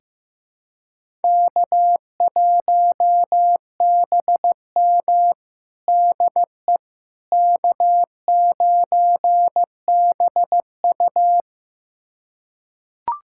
Morse code, heard as a keyed tone, about 15 wpm, 700 Hz. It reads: K1BM DE K9BU
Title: USA Callsign DE Callsign - 4 Characters - 15wpm